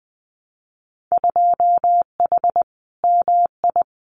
Morse code, heard as a keyed tone, about 20 wpm, 700 Hz. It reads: 2H MI
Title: Winter Field Day Exchanges Encore 20wpm